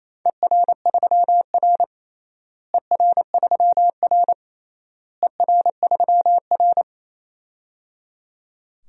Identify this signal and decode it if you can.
Morse code, heard as a keyed tone, about 28 wpm, 700 Hz. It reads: ER3R ER3R ER3R
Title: DXCC-100 Call Signs - 4 Characters - Repeated 3x - 28wpm